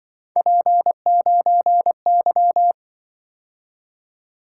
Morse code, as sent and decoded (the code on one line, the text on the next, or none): .--. ----. -.--
P9Y